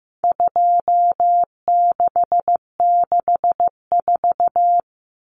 Morse code, heard as a keyed tone, about 15 wpm, 700 Hz. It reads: 2664
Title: Common Cut Numbers - 4-Digit 15wpm